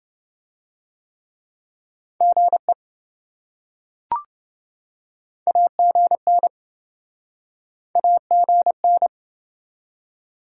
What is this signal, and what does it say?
Morse code, heard as a keyed tone, about 25 wpm, 700 Hz. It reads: GE AGN AGN